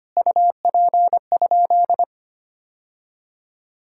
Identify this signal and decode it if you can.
Morse code, heard as a keyed tone, about 25 wpm, 700 Hz. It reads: UP?
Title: Sentences from Top 1600 Words 25wpm